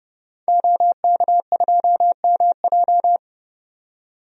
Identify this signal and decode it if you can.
Morse code, heard as a keyed tone, about 30 wpm, 700 Hz. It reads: OK2MJ
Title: DXCC-100 Call Signs - 5 Characters - 30wpm